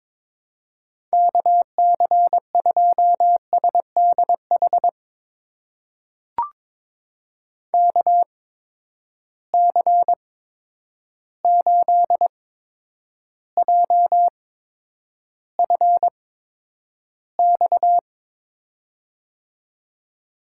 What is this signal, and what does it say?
Morse code, heard as a keyed tone, about 22 wpm, spaced as for 9 wpm, 700 Hz. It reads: KC2SDH KC8JFX
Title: USA Call Signs - 8x Character Spacing - 22wpm